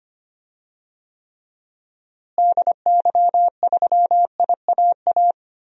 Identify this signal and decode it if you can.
Morse code, heard as a keyed tone, about 25 wpm, 700 Hz. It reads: DY3IAA
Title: DXCC-100 Call Signs - 6 Characters - Repeated 3x - 25wpm